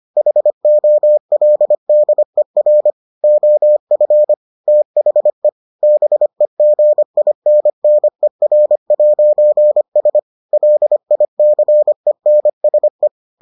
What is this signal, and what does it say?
Morse code, heard as a keyed tone, about 25 wpm, 600 Hz. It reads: HOLDER OF THE BEGINNER'S LICENSE